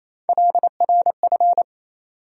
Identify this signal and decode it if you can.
Morse code, heard as a keyed tone, about 28 wpm, 700 Hz. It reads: LRF
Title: USA Call Sign Postfixes - 3 Letters 28wpm